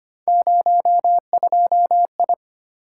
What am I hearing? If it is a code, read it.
Morse code, heard as a keyed tone, about 25 wpm, 700 Hz. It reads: 02I
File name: Three Letter-Number Combinations - 25wpm